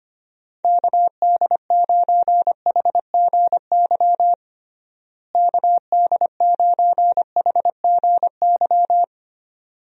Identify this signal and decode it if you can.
Morse code, heard as a keyed tone, about 25 wpm, 700 Hz. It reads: KD9HGY KD9HGY
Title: USA Call Signs - 6 Characters - Repeated 2x - 25wpm